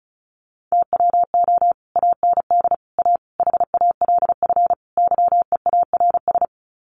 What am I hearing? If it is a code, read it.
Morse code, heard as a keyed tone, about 34 wpm, 700 Hz. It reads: TWO AND A HALF YEARS
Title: Sets of 5 Words 35wpm